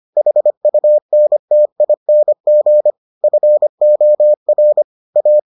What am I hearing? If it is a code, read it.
Morse code, heard as a keyed tone, about 25 wpm, 600 Hz. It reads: HUNTING FOR A